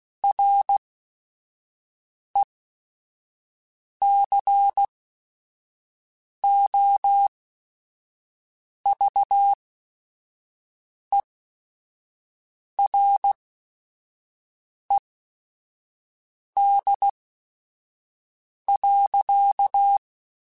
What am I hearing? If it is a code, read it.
Morse code, heard as a keyed tone, about 16 wpm, spaced as for 5 wpm, 800 Hz. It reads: RECOVERED.